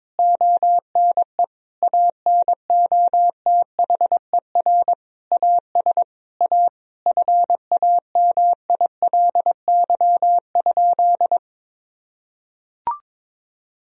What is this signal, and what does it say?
Morse code, heard as a keyed tone, about 22 wpm, 700 Hz. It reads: ONE ANOTHER AS A FAMILY?
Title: Sentences from Top 1000 Words - Review Part 1 - 22wpm